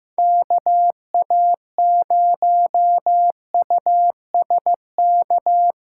Morse code, heard as a keyed tone, about 18 wpm, 700 Hz. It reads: KA0USK
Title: Speed Racing - USA 2x3 Call Signs 15wpm